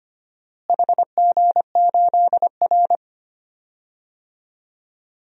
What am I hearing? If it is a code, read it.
Morse code, heard as a keyed tone, about 25 wpm, 700 Hz. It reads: HG8R
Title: DXCC-100 Call Signs - 4 Characters - Repeated 3x - 25wpm